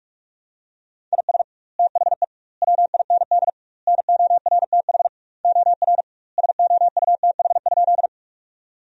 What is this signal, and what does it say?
Morse code, heard as a keyed tone, about 44 wpm, 700 Hz. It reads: IS THE WIND NORTH OR SOUTH?
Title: 2x Word-Spacing - Sentences from Top 400 Words - 45wpm